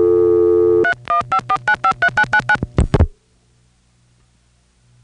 DTMF, heard as telephone touch tones, tones of about 80 ms. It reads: 3154953999